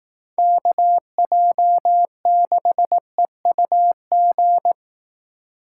Morse code, heard as a keyed tone, about 18 wpm, 700 Hz. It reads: KJ6EUG
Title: Speed Racing - USA 2x3 Call Signs 15wpm